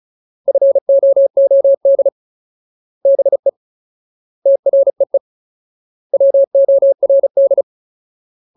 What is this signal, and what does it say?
Morse code, heard as a keyed tone, about 34 wpm, 550 Hz. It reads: FOOD BE TREE WORD
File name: Ultimate Word List / 35WPM Ultimate